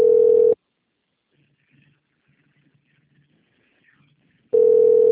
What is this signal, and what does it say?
Ringback tone (North America), 440 + 480 Hz.
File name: rg-1600-unknown- - - .3124